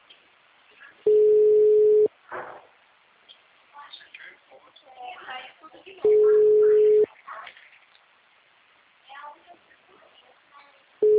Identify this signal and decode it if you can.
Ringback tone (Europe, 425 Hz), with speech.